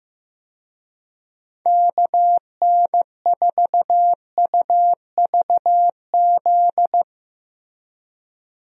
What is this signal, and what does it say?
Morse code, heard as a keyed tone, about 15 wpm, 700 Hz. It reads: KN4UVZ
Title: USA Call Signs - 6 Characters - Repeated 2x - 15wpm